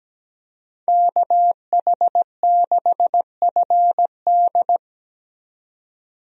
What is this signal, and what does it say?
Morse code, heard as a keyed tone, about 17 wpm, 700 Hz. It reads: KH6FD
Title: USA Call Signs - 5 Characters - 17wpm